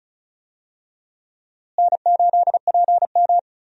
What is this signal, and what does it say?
Morse code, heard as a keyed tone, about 36 wpm, 700 Hz. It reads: N8PM
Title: USA Call Signs - 4 Characters - Repeated 2x - 35wpm